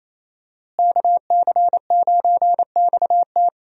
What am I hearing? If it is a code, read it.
Morse code, heard as a keyed tone, about 27 wpm, 700 Hz. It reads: KC9XT
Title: USA Call Signs - 5 Characters - Repeated 2x - 28wpm